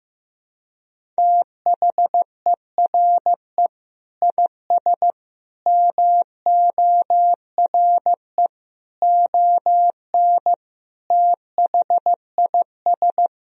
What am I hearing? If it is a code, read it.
Morse code, heard as a keyed tone, about 15 wpm, 700 Hz. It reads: THERE IS MORE ON THIS
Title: Sentences from Top 1000 Words - Review Part 1 - 15wpm